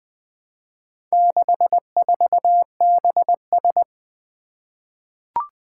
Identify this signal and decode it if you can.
Morse code, heard as a keyed tone, about 20 wpm, 700 Hz. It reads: 64BS
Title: Four Challenging Letter-Number Combinations 20wpm